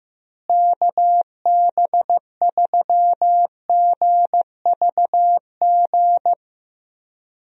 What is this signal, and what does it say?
Morse code, heard as a keyed tone, about 15 wpm, 700 Hz. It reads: KB3GVG
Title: USA Call Signs - 6 Characters - Repeated 3x - 15wpm